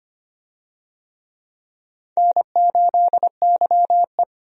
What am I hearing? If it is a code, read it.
Morse code, heard as a keyed tone, about 25 wpm, 700 Hz. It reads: N8YE
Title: USA Call Signs - 4 Characters - 25wpm